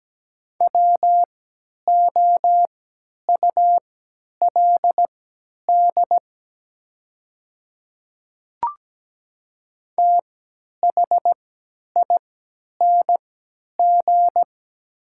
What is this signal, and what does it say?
Morse code, heard as a keyed tone, about 17 wpm, spaced as for 10 wpm, 700 Hz. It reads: WOULD THING
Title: 3x Character Spacing - Top 100 Words - 17wpm